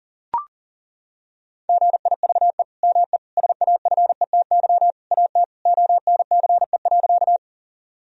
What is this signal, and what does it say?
Morse code, heard as a keyed tone, about 40 wpm, 700 Hz. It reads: GIVE ME SAFETY AT ONCE.
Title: Sentences from Top 1000 Words - Review Part 4 - 40wpm